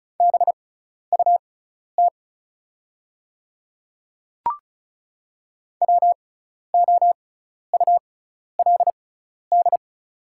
Morse code, heard as a keyed tone, about 36 wpm, spaced as for 12 wpm, 690 Hz. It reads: BUT WOULD
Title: 6x Character Spacing - Top 200 Words - 35wpm